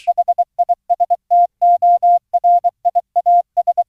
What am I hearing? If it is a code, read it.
Morse code, heard as a keyed tone, about 23 wpm, 700 Hz. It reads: HISTORIAS